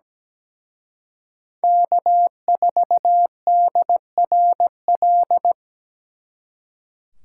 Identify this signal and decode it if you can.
Morse code, heard as a keyed tone, about 20 wpm, 700 Hz. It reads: K4DRL
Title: Speed Racing - USA 1x3 Call Signs 17wpm